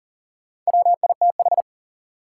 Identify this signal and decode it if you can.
Morse code, heard as a keyed tone, about 40 wpm, 700 Hz. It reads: WITH
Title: Sentences from Top 100 Words - New Word Drill - 40wpm